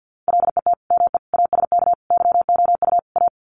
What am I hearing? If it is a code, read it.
Morse code, heard as a keyed tone, about 50 wpm, 700 Hz. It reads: LET ME ASK YOU A
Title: Sets of 5 Words Encore 50wpm